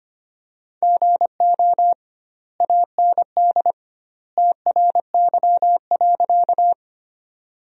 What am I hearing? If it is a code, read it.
Morse code, heard as a keyed tone, about 25 wpm, 700 Hz. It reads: GO AND TRY.
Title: 2x Word-Spacing - Sentences from Top 200 Words - 25wpm